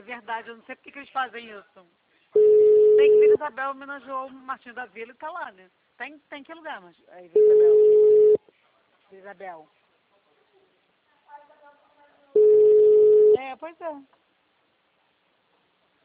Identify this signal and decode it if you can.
Ringback tone (Europe, 425 Hz), with speech.
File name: out- -2021- - - .51049